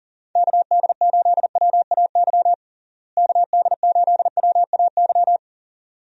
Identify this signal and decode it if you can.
Morse code, heard as a keyed tone, about 37 wpm, 700 Hz. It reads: KD8WAY KD8WAY
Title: USA Call Signs - 6 Characters - Repeated 2x - 40wpm